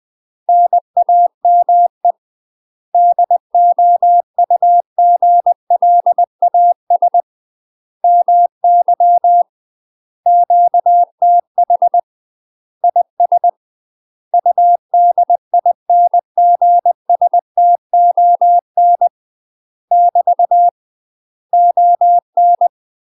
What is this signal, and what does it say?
Morse code, heard as a keyed tone, about 20 wpm, 700 Hz. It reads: NAME DOUGLAS MY QTH IS UDINGSTON <BT> ON
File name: UK MASTER QSOs TRAINING/20WPM